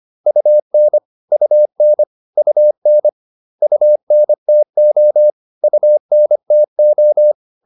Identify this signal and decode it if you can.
Morse code, heard as a keyed tone, about 25 wpm, 600 Hz. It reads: UN UN UN UNTO UNTO
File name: UN4 25